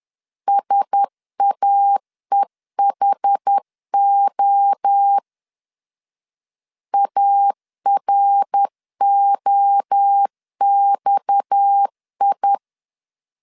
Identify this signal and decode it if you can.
Morse code, heard as a keyed tone, about 10 wpm, 800 Hz. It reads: SAEHO AROXI
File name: LEZIONI / 6 PKX0 / 6 PKX0 10wpm 2707